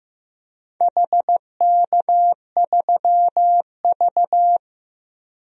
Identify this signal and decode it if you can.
Morse code, heard as a keyed tone, about 15 wpm, 700 Hz. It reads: HK3V
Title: DXCC-100 Call Signs - 4 Characters - 15wpm